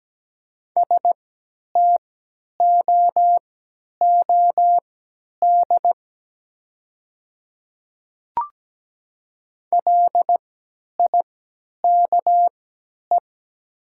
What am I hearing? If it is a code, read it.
Morse code, heard as a keyed tone, about 17 wpm, spaced as for 10 wpm, 690 Hz. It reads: STOOD LIKE
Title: 3x Character Spacing - Top 500 Words - 17wpm